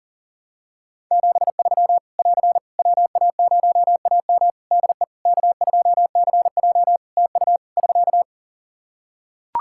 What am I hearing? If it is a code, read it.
Morse code, heard as a keyed tone, about 40 wpm, 700 Hz. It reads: 73 <AR> WA0AM DE K2CJ TU <SK>
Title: K1USN SST QSO Practice 40wpm